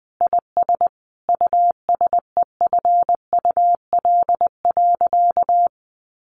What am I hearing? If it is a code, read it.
Morse code, heard as a keyed tone, about 20 wpm, 700 Hz. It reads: IS USEFUL.
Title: Sentences from Top 1500 Words 20wpm